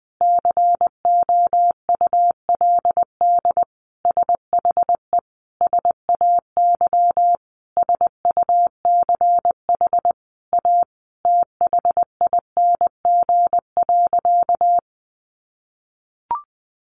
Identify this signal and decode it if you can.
Morse code, heard as a keyed tone, about 20 wpm, 700 Hz. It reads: COULD SHE SAY SUCH A THING.
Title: Sentences from Top 200 Words 20wpm